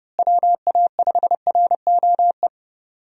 Morse code, heard as a keyed tone, about 32 wpm, 700 Hz. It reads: WA5ROE
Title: USA Call Signs - 6 Characters - Repeated 3x - 30wpm